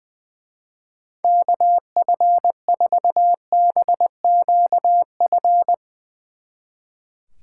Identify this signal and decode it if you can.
Morse code, heard as a keyed tone, about 20 wpm, 700 Hz. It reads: KF4BQF
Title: Speed Racing - USA 2x3 Call Signs 20wpm